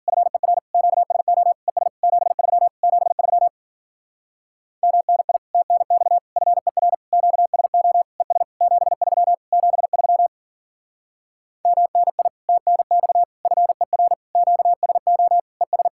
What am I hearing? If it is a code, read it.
Morse code, heard as a keyed tone, about 51 wpm, 700 Hz. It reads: FER QSO ES 73 73 MNI TNX FER QSO ES 73 73 MNI TNX FER QSO ES